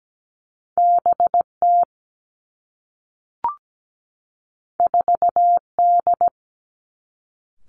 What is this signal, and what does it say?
Morse code, heard as a keyed tone, about 17 wpm, 700 Hz. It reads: BT 4D
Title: Two Challenging Letter-Number Combinations 17wpm